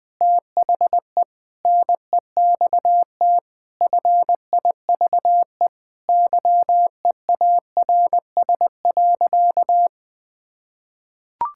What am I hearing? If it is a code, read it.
Morse code, heard as a keyed tone, about 20 wpm, 700 Hz. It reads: THE NEXT FIVE YEARS.
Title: Sentences with 5-Word Phrases - 20wpm